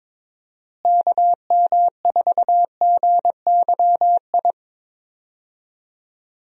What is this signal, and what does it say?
Morse code, heard as a keyed tone, about 22 wpm, 700 Hz. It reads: KM4GYI
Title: USA Call Signs - 6 Characters - Repeated 2x - 22wpm